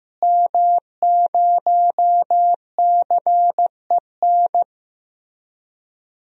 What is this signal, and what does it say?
Morse code, heard as a keyed tone, about 15 wpm, 700 Hz. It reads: M0CEN
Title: DXCC-100 Call Signs - 5 Characters - Repeated 2x - 15wpm